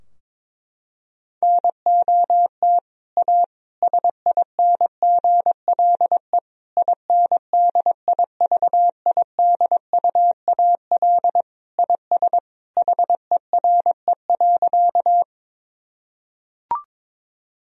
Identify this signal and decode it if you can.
Morse code, heard as a keyed tone, about 22 wpm, 700 Hz. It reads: NOT A SINGLE INDIVIDUAL IS HERE.
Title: Sentences from Top 1000 Words - Review Part 4 - 22wpm